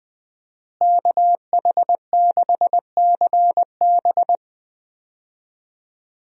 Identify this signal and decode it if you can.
Morse code, heard as a keyed tone, about 20 wpm, 700 Hz. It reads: KH6CB
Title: USA Call Signs - 5 Characters - Repeated 2x - 20wpm